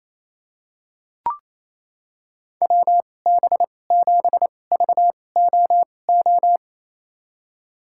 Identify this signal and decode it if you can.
Morse code, heard as a keyed tone, about 28 wpm, spaced as for 20 wpm, 700 Hz. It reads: WB7VOO
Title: USA Call Signs - 6 Characters - 2x Character Spacing - 28wpm